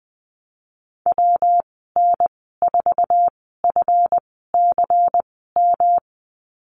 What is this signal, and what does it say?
Morse code, heard as a keyed tone, about 20 wpm, spaced as for 15 wpm, 700 Hz. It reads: WN4FCM
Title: 147 Lesson 24g - Letter y - Call Signs 20wpm